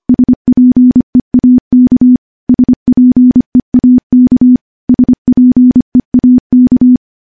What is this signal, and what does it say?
Morse code, heard as a keyed tone, about 25 wpm, 250 Hz. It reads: SPEAK SPEAK SPEAK
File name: morse / morse-training-25WPM